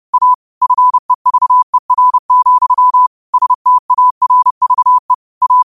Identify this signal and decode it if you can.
Morse code, heard as a keyed tone, about 30 wpm, 1000 Hz. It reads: A FEVER, STARVE A
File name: qotdfiles / 30wpm